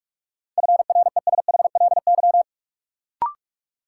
Morse code, heard as a keyed tone, about 42 wpm, 700 Hz. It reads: FRESHLY
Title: Sets of 5 Words Encore - New Word Drill - 45wpm